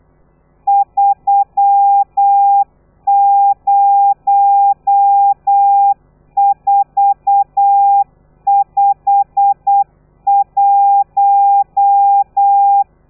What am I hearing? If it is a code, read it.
Morse code, heard as a keyed tone, about 8 wpm, 800 Hz. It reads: 30451